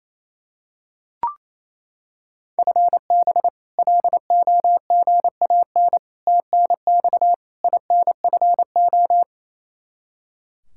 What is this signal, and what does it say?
Morse code, heard as a keyed tone, about 28 wpm, 700 Hz. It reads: FB LOGAN TNX INFO
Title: K1USN SST QSO Practice 28wpm